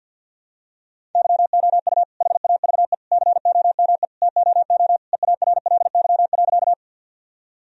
Morse code, heard as a keyed tone, about 46 wpm, 700 Hz. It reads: YOU HAVE COME TOO EARLY.
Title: Sentences from Top 1000 Words - Review Part 2 - 50wpm